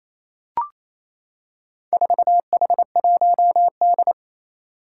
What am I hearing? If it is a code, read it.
Morse code, heard as a keyed tone, about 28 wpm, 700 Hz. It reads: 4H1D
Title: Four Challenging Letter-Number Combinations 28wpm